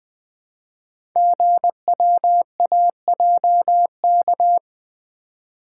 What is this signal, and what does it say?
Morse code, heard as a keyed tone, about 20 wpm, 700 Hz. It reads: GWAJK
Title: Code Groups - 5 Letter:Numbers - 20wpm